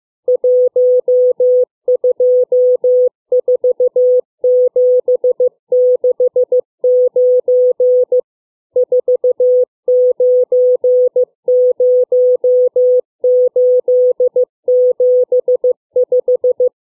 Morse code, heard as a keyed tone, about 15 wpm, 500 Hz. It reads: 124769 490875
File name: CW / letter number groups 2